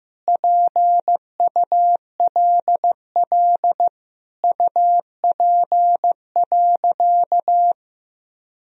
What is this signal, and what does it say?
Morse code, heard as a keyed tone, about 16 wpm, 700 Hz. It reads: PULL UP.
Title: Sentences from Top 1000 Words - Review Part 2 - 15wpm